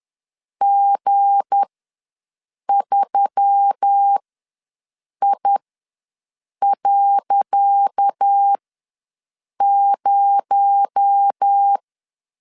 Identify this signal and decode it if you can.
Morse code, heard as a keyed tone, about 10 wpm, spaced as for 6 wpm, 800 Hz. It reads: G3I.0
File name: LEZIONI / 10 SIGNS / 10C DOT ws 10wpm 4213